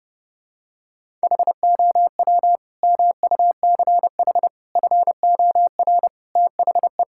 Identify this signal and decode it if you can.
Morse code, heard as a keyed tone, about 32 wpm, 700 Hz. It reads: HOW MUCH FOR THE
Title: Sentences from Top 1000 Words - Review Part 5 - 30wpm